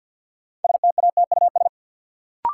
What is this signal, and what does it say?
Morse code, heard as a keyed tone, about 52 wpm, 700 Hz. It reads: STATUS
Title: Sentences from Top 1400 Words - New Word Drill - 50wpm